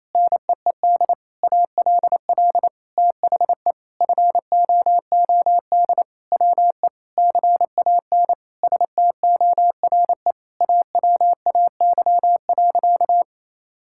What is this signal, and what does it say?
Morse code, heard as a keyed tone, about 28 wpm, 700 Hz. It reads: NEED ALL THE FOOD WE CAN STORE AWAY.
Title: Sentences from Top 1000 Words - Review Part 5 - 28wpm